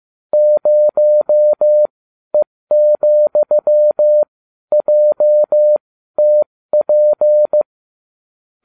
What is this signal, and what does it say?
Morse code, heard as a keyed tone, about 15 wpm, spaced as for 12 wpm, 600 Hz. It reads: 0E,JTP